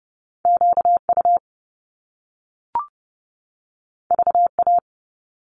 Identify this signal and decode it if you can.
Morse code, heard as a keyed tone, about 30 wpm, 700 Hz. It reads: QU VA
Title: Top Two Letter Combinations in Words 30wpm